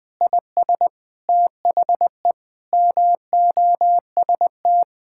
Morse code, heard as a keyed tone, about 20 wpm, 700 Hz. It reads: IS THE MOST
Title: Sentences with 9-Word Phrases - 20wpm